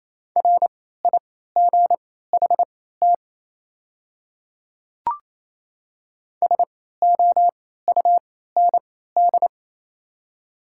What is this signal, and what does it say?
Morse code, heard as a keyed tone, about 28 wpm, spaced as for 16 wpm, 700 Hz. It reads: RIGHT SOUND
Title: 3x Character Spacing - Top 500 Words - 28wpm